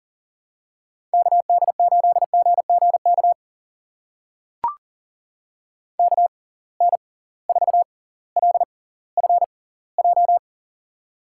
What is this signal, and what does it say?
Morse code, heard as a keyed tone, about 39 wpm, spaced as for 20 wpm, 700 Hz. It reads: KD8GGK KN4LFJ
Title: USA Call Signs - 6 Characters - 6x Character Spacing - 40wpm